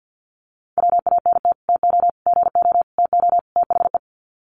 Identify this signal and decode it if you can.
Morse code, heard as a keyed tone, about 50 wpm, 700 Hz. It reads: WANT TO GO TO THE
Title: Sets of 5 Words Encore2 50wpm